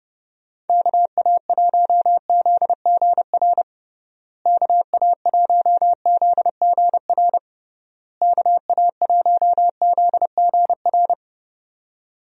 Morse code, heard as a keyed tone, about 30 wpm, 700 Hz. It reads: KA1ZGR KA1ZGR KA1ZGR
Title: USA Call Signs - 6 Characters - Repeated 3x - 30wpm